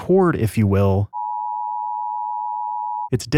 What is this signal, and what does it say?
Emergency Alert System (SAME) tones (the 853 and 960 Hz attention tone), with speech. It is heard from 1.0 to 3.1 s.